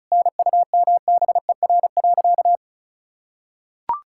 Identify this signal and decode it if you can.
Morse code, heard as a keyed tone, about 36 wpm, 700 Hz. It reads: NUMBER.
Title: Sentences from Top 1000 Words - Review Part 6 - 35wpm